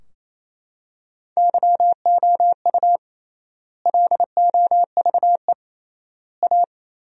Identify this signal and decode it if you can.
Morse code, heard as a keyed tone, about 28 wpm, 700 Hz. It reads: YOU LOVE A
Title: 3x Word-Spacing - Sentences from Top 500 Words - 28wpm